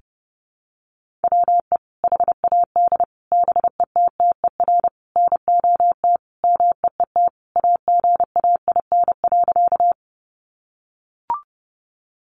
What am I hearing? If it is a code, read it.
Morse code, heard as a keyed tone, about 30 wpm, 700 Hz. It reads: WE HAD BETTER NOT MEET AGAIN.
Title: Sentences from Top 1100 Words 30wpm